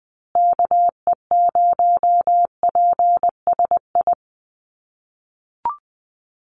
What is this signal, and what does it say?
Morse code, heard as a keyed tone, about 20 wpm, 700 Hz. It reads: KE0PSI
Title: USA Call Signs Repeated 2x 20wpm